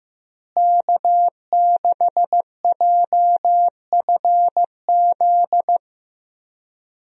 Morse code, heard as a keyed tone, about 15 wpm, 700 Hz. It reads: K6JFZ
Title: USA Call Signs - 5 Characters - Repeated 2x - 15wpm